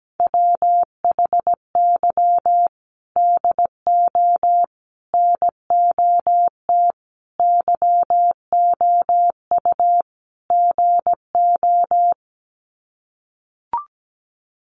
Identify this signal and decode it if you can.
Morse code, heard as a keyed tone, about 17 wpm, 700 Hz. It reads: WHY DO NOT YOU GO
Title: Sets of 5 Words Encore 17wpm